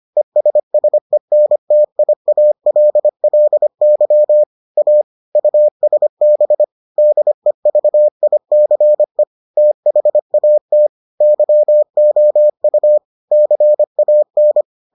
Morse code, heard as a keyed tone, about 25 wpm, 600 Hz. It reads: ESSENTIALLY A USB DEVICE THAT YOU CAN